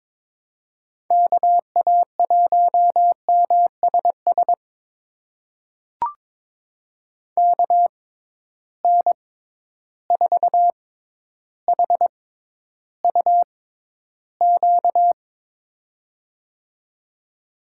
Morse code, heard as a keyed tone, about 22 wpm, spaced as for 11 wpm, 700 Hz. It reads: KA1MSS KN4HUQ
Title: USA Call Signs - 6 Characters - 6x Character Spacing - 22wpm